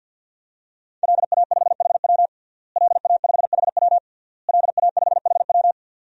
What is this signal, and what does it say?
Morse code, heard as a keyed tone, about 51 wpm, 700 Hz. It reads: LA5HW LA5HW LA5HW
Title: DXCC-100 Call Signs - 5 Characters - Repeated 3x - 50wpm